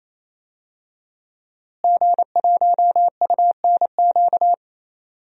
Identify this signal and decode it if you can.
Morse code, heard as a keyed tone, about 28 wpm, 700 Hz. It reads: G1UNQ
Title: DXCC-100 Call Signs - 5 Characters - Repeated 3x - 28wpm